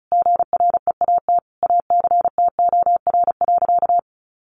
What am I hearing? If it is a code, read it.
Morse code, heard as a keyed tone, about 34 wpm, 700 Hz. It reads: GREAT ACTOR.
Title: Sentences from Top 1500 Words 35wpm